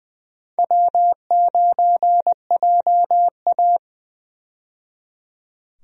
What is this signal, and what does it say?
Morse code, heard as a keyed tone, about 20 wpm, 700 Hz. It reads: W9JA
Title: USA Call Signs - 4 Characters - 20wpm